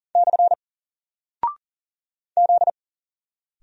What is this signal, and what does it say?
Morse code, heard as a keyed tone, about 39 wpm, 700 Hz. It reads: / Z
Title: Single Character (Letter:Number:Punctuation:Prosign) 40wpm